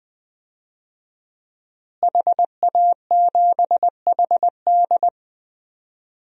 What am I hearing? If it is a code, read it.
Morse code, heard as a keyed tone, about 20 wpm, 700 Hz. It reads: HA7HD
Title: DXCC-100 Call Signs - 5 Characters - Repeated 3x - 20wpm